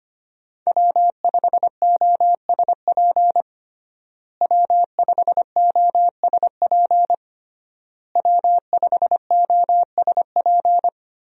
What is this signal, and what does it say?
Morse code, heard as a keyed tone, about 25 wpm, 700 Hz. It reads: W5OSP W5OSP W5OSP
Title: USA Call Signs - 5 Characters - Repeated 3x - 25wpm